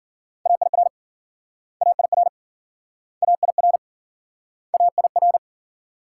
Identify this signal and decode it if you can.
Morse code, heard as a keyed tone, about 53 wpm, 700 Hz. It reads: AIR AIR AIR AIR